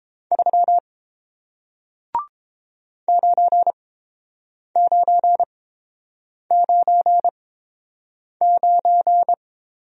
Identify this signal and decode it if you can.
Morse code, heard as a keyed tone, about 29 wpm, spaced as for 7 wpm, 700 Hz. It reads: 3 9999